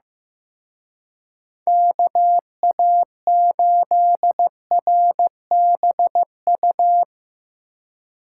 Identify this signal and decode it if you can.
Morse code, heard as a keyed tone, about 16 wpm, 700 Hz. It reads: KA8RBU
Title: Speed Racing - USA 2x3 Call Signs 15wpm